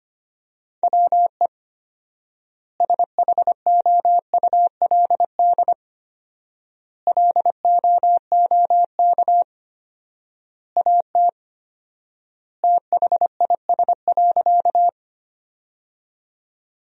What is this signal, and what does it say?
Morse code, heard as a keyed tone, about 25 wpm, 700 Hz. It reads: WE SHOULD LOOK AT THIS.
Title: 4x Word-Spacing - Sentences from Top 300 Words - 25wpm